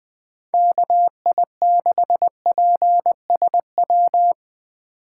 Morse code, heard as a keyed tone, about 20 wpm, 700 Hz. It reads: KI6PSW
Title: USA Call Signs - 6 Characters - 8x Character Spacing - 20wpm